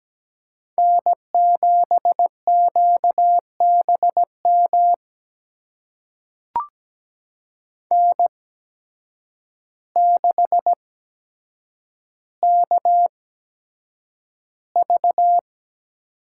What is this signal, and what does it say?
Morse code, heard as a keyed tone, about 17 wpm, spaced as for 8 wpm, 700 Hz. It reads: N7QBM N6KV
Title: USA Call Signs - 5 Characters - 8x Character Spacing - 17wpm